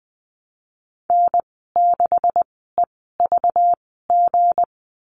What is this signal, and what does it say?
Morse code, heard as a keyed tone, about 20 wpm, spaced as for 14 wpm, 700 Hz. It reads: N6EVG
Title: 245 Lesson 39g - Letter z - Call Signs 20wpm